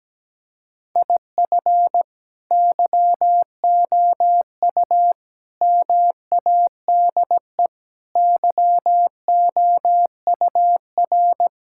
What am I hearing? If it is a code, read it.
Morse code, heard as a keyed tone, about 17 wpm, 700 Hz. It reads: IF YOU MADE YOUR
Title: Sentences from Top 1600 Words 17wpm